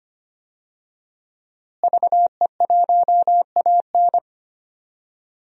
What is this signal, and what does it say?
Morse code, heard as a keyed tone, about 25 wpm, 700 Hz. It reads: VE1AN
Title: DXCC-100 Call Signs - 5 Characters - Repeated 2x - 25wpm